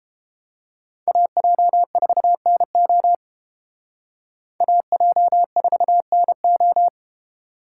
Morse code, heard as a keyed tone, about 31 wpm, 700 Hz. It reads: AJ4NO AJ4NO